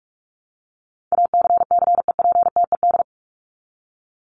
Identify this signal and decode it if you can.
Morse code, heard as a keyed tone, about 44 wpm, 700 Hz. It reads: ACCEPTED
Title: ED Suffix - Top 100 Words with Base Words 45wpm